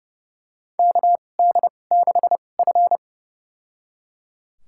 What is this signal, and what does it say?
Morse code, heard as a keyed tone, about 31 wpm, spaced as for 22 wpm, 700 Hz. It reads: KD6F
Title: USA Call Signs - 4 Characters - 2x Character Spacing - 30wpm